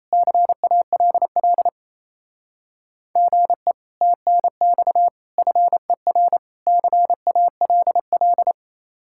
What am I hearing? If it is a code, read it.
Morse code, heard as a keyed tone, about 37 wpm, 700 Hz. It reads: CALL GE TNX FER CALL